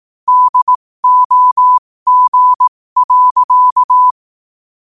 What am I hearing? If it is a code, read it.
Morse code, heard as a keyed tone, about 18 wpm, spaced as for 15 wpm, 1000 Hz. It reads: DOG.
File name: qotdfiles / 15wpm